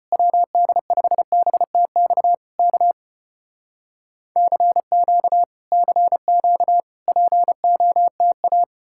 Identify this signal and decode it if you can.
Morse code, heard as a keyed tone, about 34 wpm, 700 Hz. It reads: WD5BTX K CQ CQ POTA